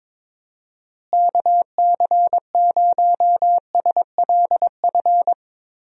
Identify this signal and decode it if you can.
Morse code, heard as a keyed tone, about 22 wpm, 700 Hz. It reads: KC0SLF
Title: USA Call Signs - 6 Characters - Repeated 3x - 22wpm